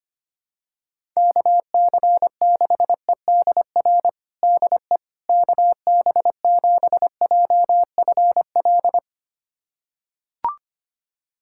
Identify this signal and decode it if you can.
Morse code, heard as a keyed tone, about 25 wpm, 700 Hz. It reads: KC6EDR DE KB7JFL
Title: USA Callsign DE Callsign 25wpm